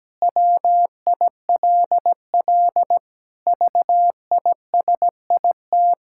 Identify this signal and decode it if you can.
Morse code, heard as a keyed tone, about 17 wpm, 700 Hz. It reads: WILL VISIT
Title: Sentences from Top 1000 Words - Review Part 6 - 17wpm